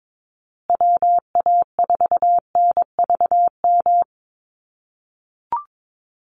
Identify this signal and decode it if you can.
Morse code, heard as a keyed tone, about 22 wpm, 700 Hz. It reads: WA4NVM
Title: USA Call Signs Repeated 2x 22wpm